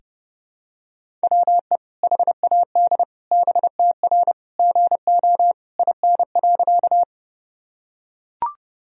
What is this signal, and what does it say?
Morse code, heard as a keyed tone, about 30 wpm, 700 Hz. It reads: WE HAD BTR GO IN.
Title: Sentences from Top 400 Words and Abbreviations 30wpm